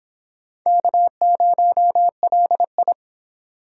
Morse code, heard as a keyed tone, about 25 wpm, 700 Hz. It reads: K0LI